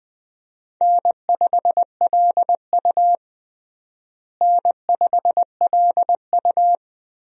Morse code, heard as a keyed tone, about 20 wpm, 700 Hz. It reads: N5LU N5LU
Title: USA Call Signs - 4 Characters - Repeated 3x - 20wpm